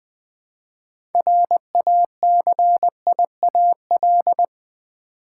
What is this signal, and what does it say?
Morse code, heard as a keyed tone, about 20 wpm, 700 Hz. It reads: RACIAL
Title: 6 Letter Words - First 3 Letters Given - 20wpm